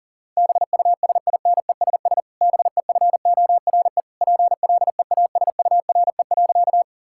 Morse code, heard as a keyed tone, about 40 wpm, 700 Hz. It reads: BUSINESS BEFORE PLEASURE.
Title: Sentences from Top 1600 Words 40wpm